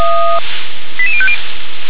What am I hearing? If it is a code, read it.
Five-tone selective call (ZVEI), heard as quick runs of five tones, tones of about 70 ms. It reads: 88050